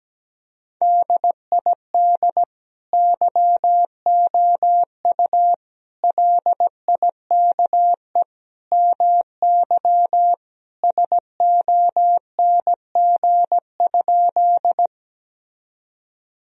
Morse code, heard as a keyed tone, about 17 wpm, 700 Hz. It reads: DID YOU LIKE MY SONG?
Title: Sentences from Top 1000 Words - Review Part 2 - 17wpm